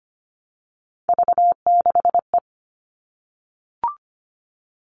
Morse code, heard as a keyed tone, about 25 wpm, 700 Hz. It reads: V6E
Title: All Cut Numbers - 3-Digit 25wpm